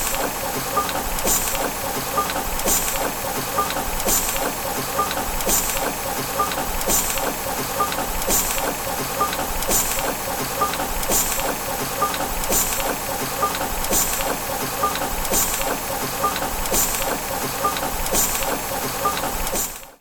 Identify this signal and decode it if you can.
There is no code.